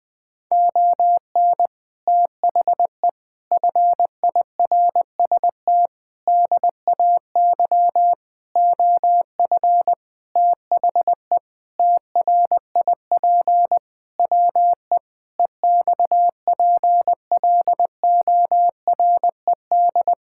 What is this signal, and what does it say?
Morse code, heard as a keyed tone, about 20 wpm, 700 Hz. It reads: ON THE FIRST DAY OF THE TRIP WE EXPLORED
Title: Sentences with 6-Word Phrases - 20wpm